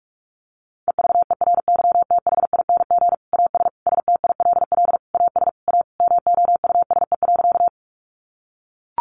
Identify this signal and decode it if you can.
Morse code, heard as a keyed tone, about 44 wpm, 700 Hz. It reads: EVERYTHING AS STILL AS A MOUSE.
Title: Sentences from Top 700 Words 45wpm